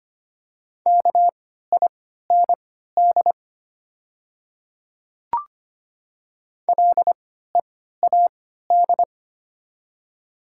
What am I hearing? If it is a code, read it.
Morse code, heard as a keyed tone, about 25 wpm, spaced as for 14 wpm, 700 Hz. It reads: KIND LEAD
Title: 3x Character Spacing - Top 500 Words - 25wpm